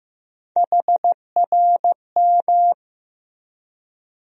Morse code, heard as a keyed tone, about 15 wpm, 700 Hz. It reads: HRM
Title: USA Call Sign Postfixes - 3 Letters 15wpm